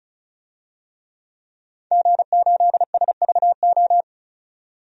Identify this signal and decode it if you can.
Morse code, heard as a keyed tone, about 34 wpm, 700 Hz. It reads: G8SVO
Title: DXCC-100 Call Signs - 5 Characters - 35wpm